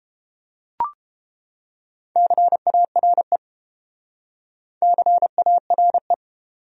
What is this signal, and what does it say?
Morse code, heard as a keyed tone, about 29 wpm, 700 Hz. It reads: CARE CARE